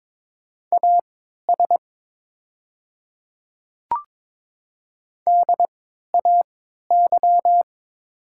Morse code, heard as a keyed tone, about 22 wpm, spaced as for 13 wpm, 700 Hz. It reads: AS DAY
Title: 3x Character Spacing - Top 200 Words - 22wpm